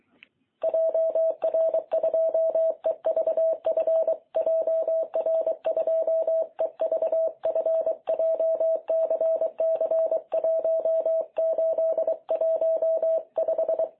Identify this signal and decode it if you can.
Morse code, heard as a keyed tone, about 24 wpm, 660 Hz. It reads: JR2EVF JR2EVF JCC1815
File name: jh8cbh / 2021cw